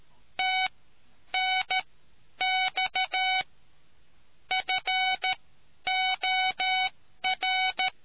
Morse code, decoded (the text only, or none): TNX FOR